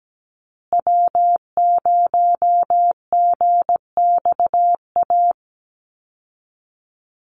Morse code, heard as a keyed tone, about 17 wpm, 700 Hz. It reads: W0GXA
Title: Call Signs Repeated 2x - 17wpm